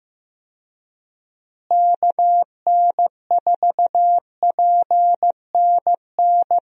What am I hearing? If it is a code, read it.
Morse code, heard as a keyed tone, about 15 wpm, 700 Hz. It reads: KN4PNN
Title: USA Call Signs - 6 Characters - Repeated 3x - 15wpm